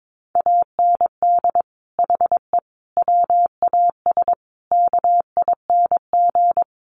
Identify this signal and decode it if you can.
Morse code, heard as a keyed tone, about 22 wpm, 700 Hz. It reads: AND HE WAS KING
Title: Sentences from Top 1400 Words 22wpm